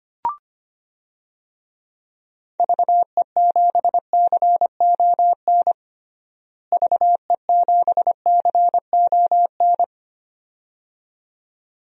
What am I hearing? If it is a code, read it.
Morse code, heard as a keyed tone, about 25 wpm, 700 Hz. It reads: VE7CON VE7CON
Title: DXCC-100 Call Signs - 6 Characters - Repeated 2x - 25wpm